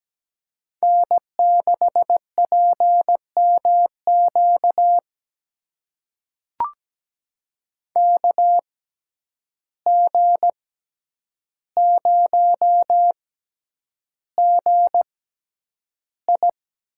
Morse code, heard as a keyed tone, about 17 wpm, spaced as for 9 wpm, 700 Hz. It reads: N6PMQ KG0GI
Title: USA Call Signs - 5 Characters - 6x Character Spacing - 17wpm